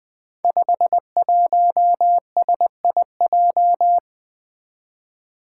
Morse code, heard as a keyed tone, about 20 wpm, 700 Hz. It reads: 51SIJ
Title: Five Challenging Letter-Number Combinations 20wpm